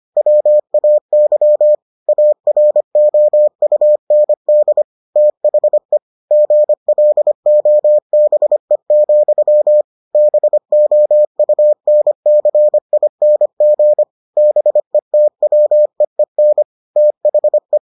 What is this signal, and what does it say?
Morse code, heard as a keyed tone, about 25 wpm, 600 Hz. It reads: WAY AROUND THE GLOBE, BOUNCING BETWEEN THE